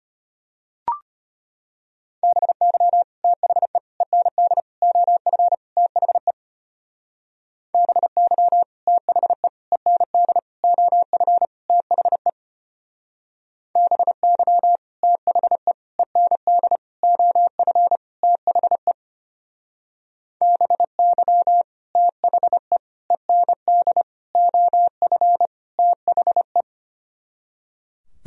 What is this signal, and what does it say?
Morse code, heard as a keyed tone, about 32 wpm, 700 Hz. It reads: BY THE END OF THE BY THE END OF THE BY THE END OF THE BY THE END OF THE